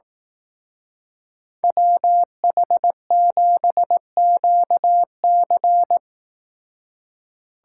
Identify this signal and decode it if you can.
Morse code, heard as a keyed tone, about 18 wpm, 700 Hz. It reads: WH7QC
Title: Speed Racing - USA 2x2 Call Signs 15wpm